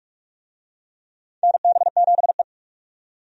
Morse code, heard as a keyed tone, about 46 wpm, 700 Hz. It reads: NB7E
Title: USA Call Signs - 4 Characters - 45wpm